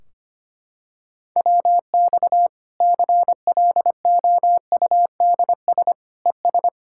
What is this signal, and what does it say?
Morse code, heard as a keyed tone, about 25 wpm, 700 Hz. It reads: WX CLOUDS ES
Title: Practice QSO - Exchange 2 - Info - 25wpm